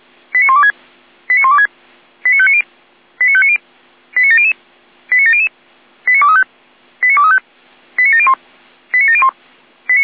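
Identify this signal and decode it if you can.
Five-tone selective call (ZVEI), heard as quick runs of five tones, tones of about 70 ms. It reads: 89126 89126 89590 89590 89700 89700 89235 89235 89701 89701
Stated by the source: ZVEI five-tone selective call received on an SDR.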